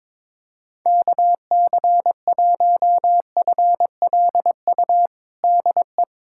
Morse code, heard as a keyed tone, about 21 wpm, 700 Hz. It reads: KC1FLU DE
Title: USA Callsign DE Callsign - 6 Characters - 22wpm